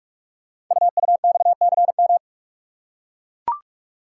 Morse code, heard as a keyed tone, about 44 wpm, 700 Hz. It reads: UUXCM
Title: Code Groups - 5 Letters - 45wpm